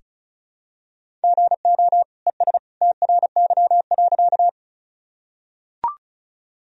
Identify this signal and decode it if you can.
Morse code, heard as a keyed tone, about 36 wpm, 700 Hz. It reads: GO ES TRY.
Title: Sentences from Top 200 Words and Abbreviations 35wpm